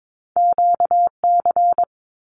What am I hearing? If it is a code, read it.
Morse code, heard as a keyed tone, about 22 wpm, 700 Hz. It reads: QC
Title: Canadian Provinces and Territories 22wpm